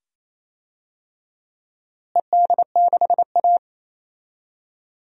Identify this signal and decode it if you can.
Morse code, heard as a keyed tone, about 27 wpm, 700 Hz. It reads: ED6A
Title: DXCC-100 Call Signs - 4 Characters - Repeated 2x - 28wpm